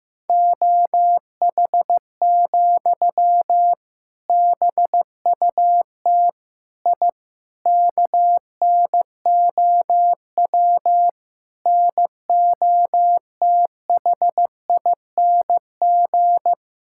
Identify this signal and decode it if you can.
Morse code, heard as a keyed tone, about 15 wpm, 700 Hz. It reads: OH, BUT I KNOW NOTHING
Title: Sentences from Top 1000 Words - Review Part 6 - 15wpm